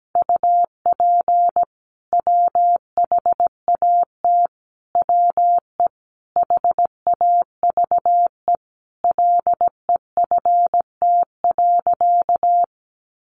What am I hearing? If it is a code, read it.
Morse code, heard as a keyed tone, about 17 wpm, 700 Hz. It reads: UP WHAT WE HAVE LEFT.
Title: Sentences from Top 1500 Words 17wpm